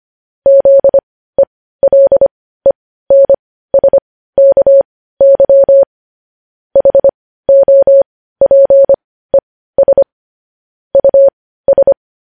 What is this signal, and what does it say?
Morse code, heard as a keyed tone, about 25 wpm, spaced as for 15 wpm, 550 Hz. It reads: ZELENSKY HOPES US
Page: News Headlines in Morse Code at 15 WPM